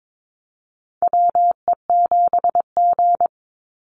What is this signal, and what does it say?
Morse code, heard as a keyed tone, about 22 wpm, 700 Hz. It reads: WE7G
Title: Call Signs Repeated 3x - 22wpm